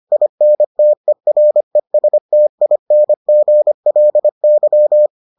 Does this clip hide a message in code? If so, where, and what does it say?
Yes: Morse code from 0.1 to 5.1 s, heard as a keyed tone, about 25 wpm, 600 Hz. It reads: INTERESTINGLY